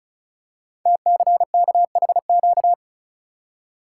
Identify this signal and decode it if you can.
Morse code, heard as a keyed tone, about 36 wpm, 690 Hz. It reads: TCKHQ
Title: Code Groups - 5 Letters - 35wpm